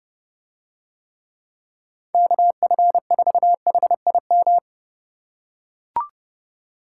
Morse code, heard as a keyed tone, about 29 wpm, 700 Hz. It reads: KF4HIM